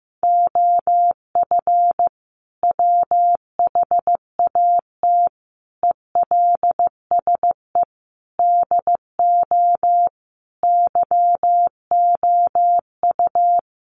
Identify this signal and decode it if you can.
Morse code, heard as a keyed tone, about 15 wpm, 700 Hz. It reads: OF WHAT ELSE DO YOU
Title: Sentences from Top 1400 Words 15wpm